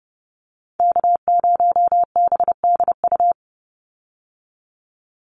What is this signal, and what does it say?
Morse code, heard as a keyed tone, about 30 wpm, 700 Hz. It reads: K0BDU
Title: USA Call Signs 30wpm